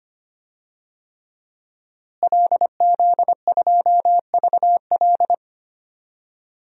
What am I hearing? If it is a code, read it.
Morse code, heard as a keyed tone, about 25 wpm, 700 Hz. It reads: LZ2VL
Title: DXCC-100 Call Signs - 5 Characters - Repeated 3x - 25wpm